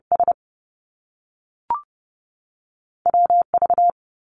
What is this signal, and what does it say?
Morse code, heard as a keyed tone, about 30 wpm, spaced as for 6 wpm, 700 Hz. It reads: SWV
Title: US States 30wpm